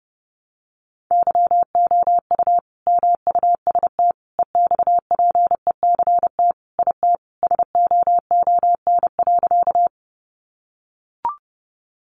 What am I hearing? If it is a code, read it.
Morse code, heard as a keyed tone, about 30 wpm, 700 Hz. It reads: YOU MUST EXPECT IT SOON.
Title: Sentences from Top 1100 Words 30wpm